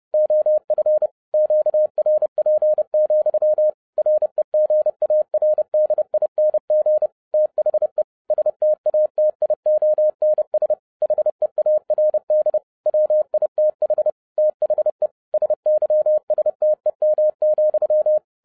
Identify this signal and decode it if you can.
Morse code, heard as a keyed tone, about 30 wpm, 610 Hz. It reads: OF QRP, REGARDING THE STATIONS HEARD WITH THE SYSTEM,